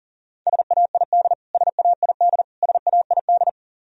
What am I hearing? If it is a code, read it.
Morse code, heard as a keyed tone, about 40 wpm, 700 Hz. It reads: SAID SAID SAID
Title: Top 100 Words Repeated 40wpm